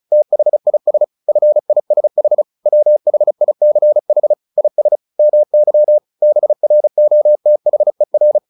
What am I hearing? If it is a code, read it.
Morse code, heard as a keyed tone, about 34 wpm, 600 Hz. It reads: THIS FISH WHICH IS MY BROTHER
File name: the old man and the sea 2 35wpm